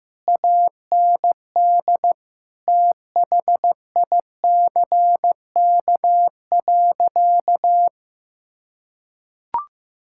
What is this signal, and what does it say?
Morse code, heard as a keyed tone, about 15 wpm, 700 Hz. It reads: AND THICK.
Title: Sentences from Top 1600 Words 15wpm